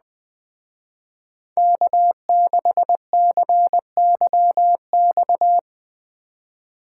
Morse code, heard as a keyed tone, about 20 wpm, 700 Hz. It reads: K6CYX